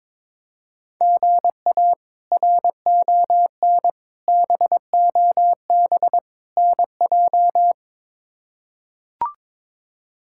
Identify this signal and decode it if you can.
Morse code, heard as a keyed tone, about 22 wpm, 700 Hz. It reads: GA RON BOB NJ
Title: K1USN SST QSO Practice - 22wpm